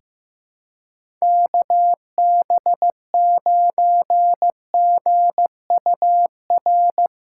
Morse code, heard as a keyed tone, about 15 wpm, 700 Hz. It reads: KB9GUR
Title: USA Callsign DE Callsign 15wpm